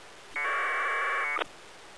Packet radio (AX.25, as in APRS).